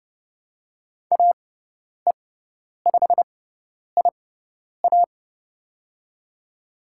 Morse code, heard as a keyed tone, about 31 wpm, spaced as for 10 wpm, 700 Hz. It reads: AE5IA
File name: Lesson 17g - Letter u - Call Signs 30wpm